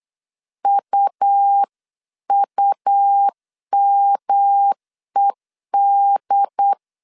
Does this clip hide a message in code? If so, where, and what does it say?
Yes: Morse code from 0.6 to 6.7 s, heard as a keyed tone, about 8 wpm, 800 Hz. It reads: UUMED